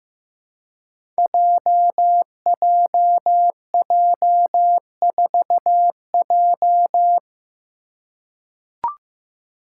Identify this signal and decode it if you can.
Morse code, heard as a keyed tone, about 15 wpm, 700 Hz. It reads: JJJ4J
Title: Five Challenging Letter-Number Combinations 15wpm